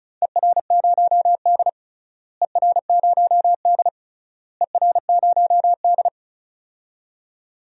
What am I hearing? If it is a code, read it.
Morse code, heard as a keyed tone, about 34 wpm, 700 Hz. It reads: ER0D ER0D ER0D
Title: DXCC-100 Call Signs - 4 Characters - Repeated 3x - 35wpm